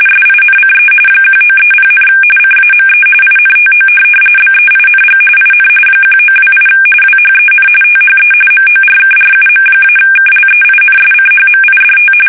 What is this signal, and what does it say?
RTTY, about 45.45 baud. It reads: SINCE 1930, CAME INTO BEING IN 1928 WHEN THE MORKRUM-KLEINSCHMIDT
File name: fsk850 / teletype-corp